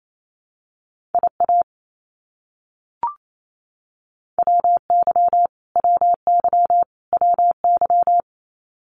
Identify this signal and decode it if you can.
Morse code, heard as a keyed tone, about 28 wpm, 700 Hz. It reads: IA WY WY WY
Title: US States Repeated 3x 28wpm